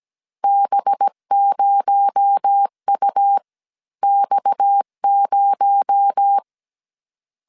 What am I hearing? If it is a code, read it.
Morse code, heard as a keyed tone, about 17 wpm, spaced as for 14 wpm, 800 Hz. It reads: B0UX0
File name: CW / DET / 1 6 1500C S A 16wpm 2635